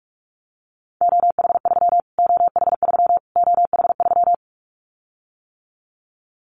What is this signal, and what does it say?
Morse code, heard as a keyed tone, about 46 wpm, 700 Hz. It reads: OH3 OH3 OH3
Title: Callsign Prefixes for Top 25 DXCC - Repeated 3x - 45wpm